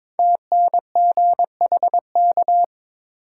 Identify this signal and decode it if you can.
Morse code, heard as a keyed tone, about 22 wpm, 700 Hz. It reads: TNGHK
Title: Code Groups - 5 Letter:Numbers - 22wpm